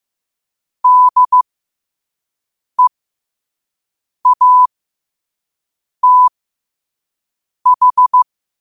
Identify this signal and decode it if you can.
Morse code, heard as a keyed tone, about 15 wpm, spaced as for 5 wpm, 1000 Hz. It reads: DEATH
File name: qotdfiles / 5wpm